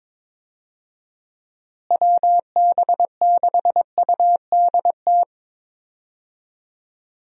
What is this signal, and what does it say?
Morse code, heard as a keyed tone, about 22 wpm, 700 Hz. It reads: WB6UDT
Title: USA Call Signs - 6 Characters - Repeated 3x - 22wpm